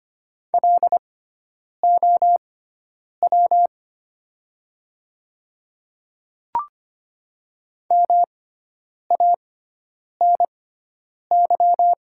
Morse code, heard as a keyed tone, about 25 wpm, spaced as for 9 wpm, 690 Hz. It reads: LOW MANY
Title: 6x Character Spacing - Top 300 Words - 25wpm